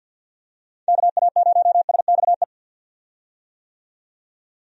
Morse code, heard as a keyed tone, about 50 wpm, 700 Hz. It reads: KA0SXE
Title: USA Call Signs - 6 Characters - 50wpm